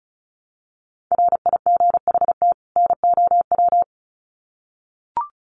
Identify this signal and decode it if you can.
Morse code, heard as a keyed tone, about 34 wpm, 700 Hz. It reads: RIGHT NOW
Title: Sets of 2 Words 35wpm